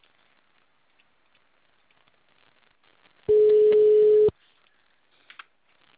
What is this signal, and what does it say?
Ringback tone (Europe, 425 Hz).
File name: out- -2002- - - .5404